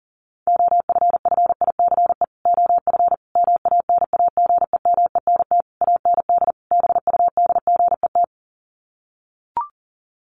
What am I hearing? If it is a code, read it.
Morse code, heard as a keyed tone, about 40 wpm, 700 Hz. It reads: OFFICE OF MANAGEMENT AND BUDGET
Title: Sets of 5 Words Encore2 40wpm